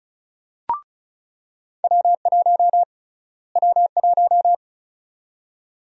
Morse code, heard as a keyed tone, about 36 wpm, 700 Hz. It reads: W1 W1
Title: USA Call Sign Prefixes - Repeated 2x - 35wpm